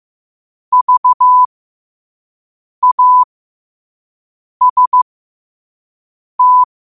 Morse code, heard as a keyed tone, about 15 wpm, spaced as for 5 wpm, 1000 Hz. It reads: VAST
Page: New quotes every day in morse code at 5 Words per minute.